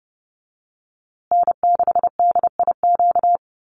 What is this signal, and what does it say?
Morse code, heard as a keyed tone, about 30 wpm, 700 Hz. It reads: N6DIQ
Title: USA Call Signs Repeated 3x 30wpm